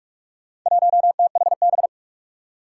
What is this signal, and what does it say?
Morse code, heard as a keyed tone, about 46 wpm, 700 Hz. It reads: 1THB
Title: Four Challenging Letter-Number Combinations 45wpm